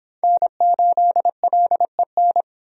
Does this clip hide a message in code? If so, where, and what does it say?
Yes: Morse code from 0.2 to 2.4 s, heard as a keyed tone, about 26 wpm, 700 Hz. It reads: N8LEN